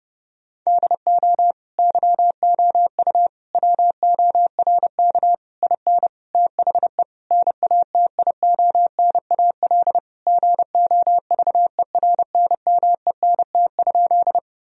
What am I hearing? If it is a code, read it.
Morse code, heard as a keyed tone, about 30 wpm, 700 Hz. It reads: DO YOU WORK IN THE NATIONAL GOVERNMENT?
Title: Sentences from Top 1000 Words - Review Part 4 - 30wpm